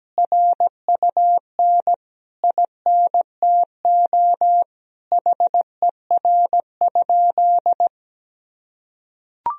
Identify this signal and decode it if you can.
Morse code, heard as a keyed tone, about 17 wpm, 700 Hz. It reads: RUN INTO HER?
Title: Sentences from Top 1000 Words - Review Part 1 - 17wpm